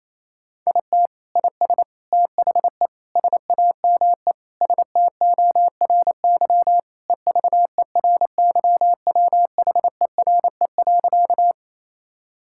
Morse code, heard as a keyed tone, about 28 wpm, 700 Hz. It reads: IT IS THE SAME STORY EVERYWHERE.
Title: Sentences from Top 1600 Words 28wpm